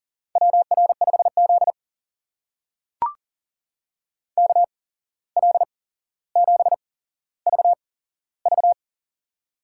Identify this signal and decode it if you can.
Morse code, heard as a keyed tone, about 38 wpm, spaced as for 16 wpm, 700 Hz. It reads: WR5Z KL7VV
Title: USA Call Signs - 8x Character Spacing - 40wpm